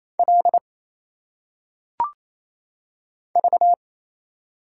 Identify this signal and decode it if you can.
Morse code, heard as a keyed tone, about 27 wpm, 700 Hz. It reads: L V
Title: Single Letters - Phonetic Pronunciation 28wpm